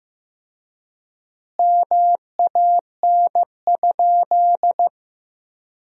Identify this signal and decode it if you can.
Morse code, heard as a keyed tone, about 16 wpm, 700 Hz. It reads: MAN?
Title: 4x Word-Spacing - Sentences from Top 200 Words - 15wpm